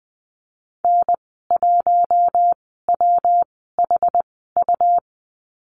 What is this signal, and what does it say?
Morse code, heard as a keyed tone, about 20 wpm, spaced as for 15 wpm, 700 Hz. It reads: N1WHU
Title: 154 Lesson 25g - Letter p - Call Signs 20wpm